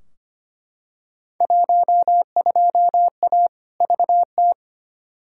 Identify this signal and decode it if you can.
Morse code, heard as a keyed tone, about 25 wpm, 700 Hz. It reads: 12A VT
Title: Field Day Exchanges Encore - 25wpm